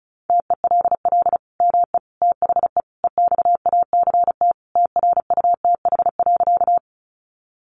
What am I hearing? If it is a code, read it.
Morse code, heard as a keyed tone, about 34 wpm, 700 Hz. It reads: TELL ME THE EXACT TRUTH.
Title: Sentences from Top 1300 Words 35wpm